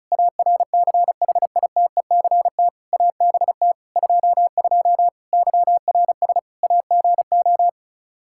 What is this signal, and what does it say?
Morse code, heard as a keyed tone, about 36 wpm, 700 Hz. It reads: ARCHITECT ABT 22 YRS AGO
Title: Practice QSO - Exchange 3 - More Info - 35wpm